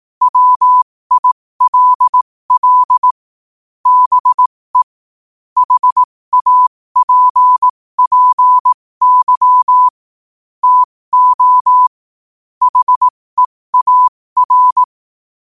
Morse code, heard as a keyed tone, about 18 wpm, spaced as for 15 wpm, 1000 Hz. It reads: WILL BE HAPPY TO HEAR
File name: qotdfiles / 15wpm